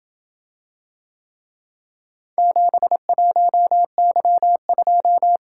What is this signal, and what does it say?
Morse code, heard as a keyed tone, about 27 wpm, 700 Hz. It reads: 71Y2